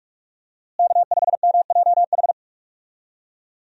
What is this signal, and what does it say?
Morse code, heard as a keyed tone, about 46 wpm, 700 Hz. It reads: K5MJH
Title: USA Call Signs - 5 Characters - Repeated 2x - 45wpm